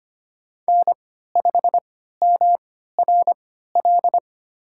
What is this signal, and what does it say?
Morse code, heard as a keyed tone, about 25 wpm, spaced as for 14 wpm, 700 Hz. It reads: N5MRL
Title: USA Call Signs - 5 Characters - 3x Character Spacing - 25wpm